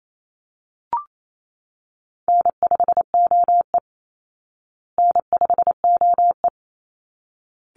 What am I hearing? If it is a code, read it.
Morse code, heard as a keyed tone, about 28 wpm, 700 Hz. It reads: N5OE N5OE
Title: Easy Call Signs Repeated 2x - 28wpm